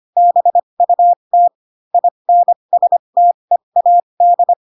Morse code, heard as a keyed tone, about 25 wpm, spaced as for 22 wpm, 700 Hz. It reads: BUT INSTEAD
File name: Pinocchio / Pinocchio ch3 18 25 ik1hge